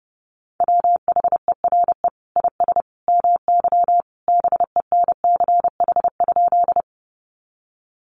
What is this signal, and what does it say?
Morse code, heard as a keyed tone, about 30 wpm, 700 Hz. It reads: WHERE IS MY BENCH?
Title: Sentences from Top 1400 Words 30wpm